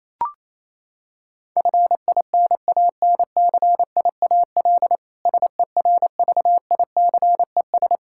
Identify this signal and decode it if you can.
Morse code, heard as a keyed tone, about 28 wpm, 700 Hz. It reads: FINANCIAL SERVICES
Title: K1USN SST QSO Practice 28wpm